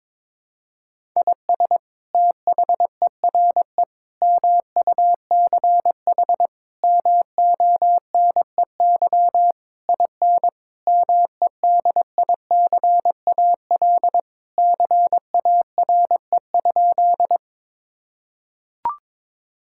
Morse code, heard as a keyed tone, about 22 wpm, 700 Hz. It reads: IS THERE MUCH MONEY IN MEDICAL CARE?
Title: Sentences from Top 1000 Words - Review Part 5 - 22wpm